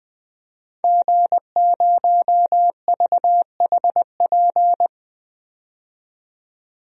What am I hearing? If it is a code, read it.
Morse code, heard as a keyed tone, about 20 wpm, 700 Hz. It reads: G0VHP
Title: DXCC-100 Call Signs - 5 Characters - 20wpm